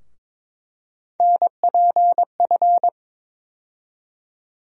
Morse code, heard as a keyed tone, about 22 wpm, 700 Hz. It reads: NPF
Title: USA Call Sign Postfixes - 3 Letters 22wpm